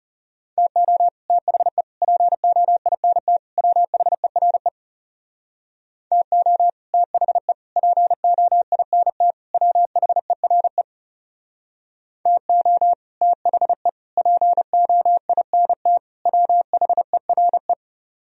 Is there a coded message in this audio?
Yes: Morse code from 0.6 to 17.7 s, heard as a keyed tone, about 40 wpm, 700 Hz. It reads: TO THE POINT WHERE TO THE POINT WHERE TO THE POINT WHERE